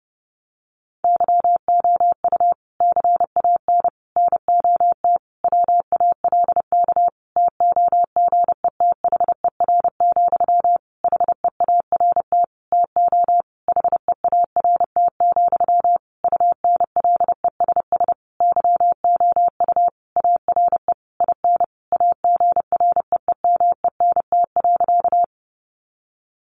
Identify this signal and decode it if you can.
Morse code, heard as a keyed tone, about 30 wpm, 700 Hz. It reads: YOU CAN NOT WALK TOGETHER, HEART TO HEART, UNLESS YOU ARE IN AGREEMENT.
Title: Sentences from Top 1500 Words 30wpm